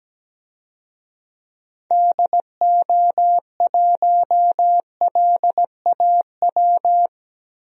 Morse code, heard as a keyed tone, about 17 wpm, 700 Hz. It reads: DO1LAW
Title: DXCC-100 Call Signs - 6 Characters - Repeated 2x - 17wpm